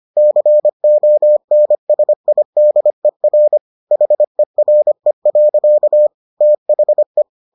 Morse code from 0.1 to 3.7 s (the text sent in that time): CONSIDER